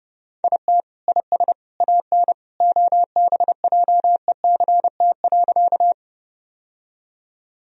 Morse code, heard as a keyed tone, about 32 wpm, 700 Hz. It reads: IT IS AN OBJECT.
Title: Sentences from Top 1000 Words - Review Part 3 - 30wpm